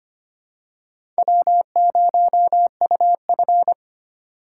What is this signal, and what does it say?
Morse code, heard as a keyed tone, about 25 wpm, 700 Hz. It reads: W0UF
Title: K1USN SST QSO Practice - 25wpm